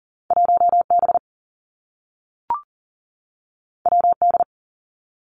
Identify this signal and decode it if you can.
Morse code, heard as a keyed tone, about 42 wpm, 700 Hz. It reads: 1 B W D
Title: Two Challenging Letter-Number Combinations 40wpm